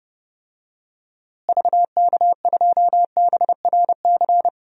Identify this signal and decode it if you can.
Morse code, heard as a keyed tone, about 32 wpm, 700 Hz. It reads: VK2BRC
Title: DXCC-100 Call Signs - 6 Characters - Repeated 3x - 30wpm